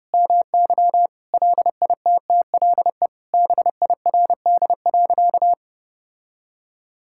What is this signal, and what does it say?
Morse code, heard as a keyed tone, about 30 wpm, 700 Hz. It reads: MY LITTLE BIRD.
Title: Sentences from Top 1000 Words - Review Part 6 - 30wpm